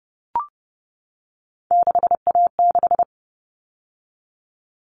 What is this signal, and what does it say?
Morse code, heard as a keyed tone, about 30 wpm, 700 Hz. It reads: 6A6
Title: All Cut Numbers - 3-Digit 30wpm